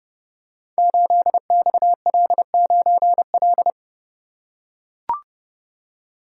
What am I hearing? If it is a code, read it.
Morse code, heard as a keyed tone, about 30 wpm, 700 Hz. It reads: 8XL9L
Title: Code Groups - 5 Letter:Numbers - 30wpm